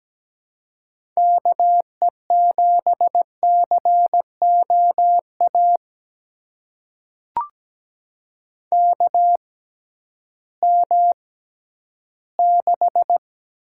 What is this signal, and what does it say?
Morse code, heard as a keyed tone, about 17 wpm, spaced as for 11 wpm, 700 Hz. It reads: KE7COA KM6
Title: USA Call Signs - 6 Characters - 6x Character Spacing - 17wpm